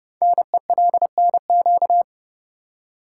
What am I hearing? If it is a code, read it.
Morse code, heard as a keyed tone, about 30 wpm, 700 Hz. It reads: NELNQ
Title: Code Groups - 5 Letters - 30wpm